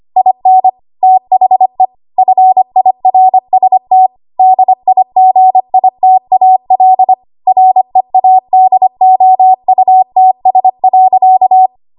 Morse code, heard as a keyed tone, about 25 wpm, 750 Hz. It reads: IN THE FIRST DIGITAL READOUTS.
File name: Archive / 25 WPM / 25WPM